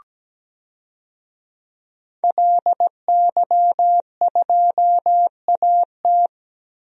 Morse code, heard as a keyed tone, about 17 wpm, 700 Hz. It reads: LY2AT
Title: DXCC-100 Call Signs - 5 Characters - Repeated 2x - 17wpm